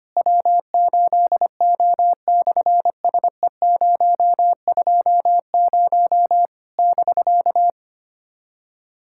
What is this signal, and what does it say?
Morse code, heard as a keyed tone, about 26 wpm, 700 Hz. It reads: W8O/SE020 <BK>
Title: SOTA QSO Practice 25wpm